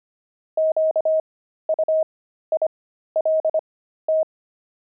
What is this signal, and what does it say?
Morse code, heard as a keyed tone, about 25 wpm, spaced as for 13 wpm, 630 Hz. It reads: QUILT
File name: pangram-fs-13-cs-25-fragment